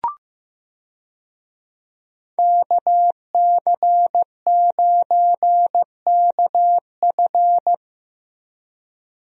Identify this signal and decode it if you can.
Morse code, heard as a keyed tone, about 15 wpm, 700 Hz. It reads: KC9KF
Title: USA Call Signs - 5 Characters - Repeated 2x - 15wpm